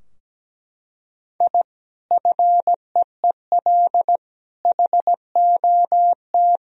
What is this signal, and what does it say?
Morse code, heard as a keyed tone, about 17 wpm, 700 Hz. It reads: I FEEL HOT
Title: Sentences from Top 1000 Words - Review Part 3 - 17wpm